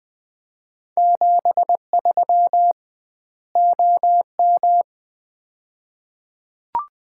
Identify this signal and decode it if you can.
Morse code, heard as a keyed tone, about 20 wpm, 700 Hz. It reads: 73 OM
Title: Common QSO Phrases - 2x Word-Spacing - 20wpm